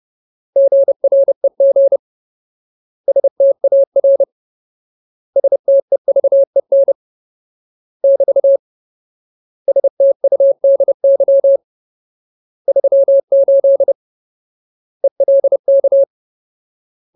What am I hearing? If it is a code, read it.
Morse code, heard as a keyed tone, about 30 wpm, 550 Hz. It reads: GREG STAR STEVEN <BT> STUDY 38 ELK
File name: Ultimate Word List / 30WPM Ultimate